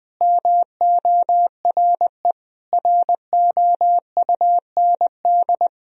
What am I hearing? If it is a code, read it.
Morse code, heard as a keyed tone, about 21 wpm, 700 Hz. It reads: MORE ROUND
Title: Sentences from Top 1000 Words - Review Part 4 - 20wpm